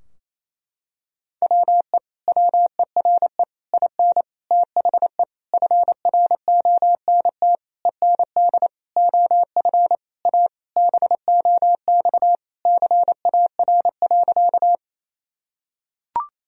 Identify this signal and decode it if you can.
Morse code, heard as a keyed tone, about 28 wpm, 700 Hz. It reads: WE WERE IN THE FRONT END OF A BOX CAR.
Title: Sentences from Top 1000 Words - Review Part 2 - 28wpm